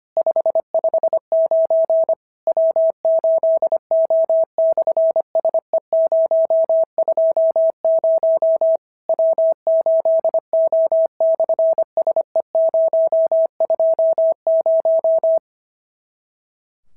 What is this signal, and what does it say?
Morse code, heard as a keyed tone, about 25 wpm, 650 Hz. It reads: 559 W8O/SE020 W8O/SE020
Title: SOTA QSO Practice 25wpm